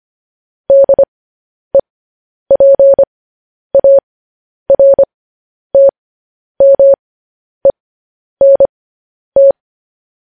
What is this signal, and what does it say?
Morse code, heard as a keyed tone, about 25 wpm, spaced as for 10 wpm, 550 Hz. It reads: DEPARTMENT
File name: podcast / News headlines-1x-25x10